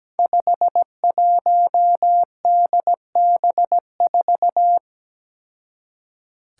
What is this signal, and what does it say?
Morse code, heard as a keyed tone, about 17 wpm, 700 Hz. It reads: 51DB4
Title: Five Challenging Letter-Number Combinations 17wpm